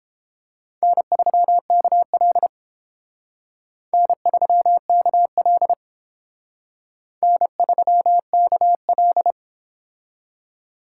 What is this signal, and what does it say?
Morse code, heard as a keyed tone, about 29 wpm, 700 Hz. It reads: N3KL N3KL N3KL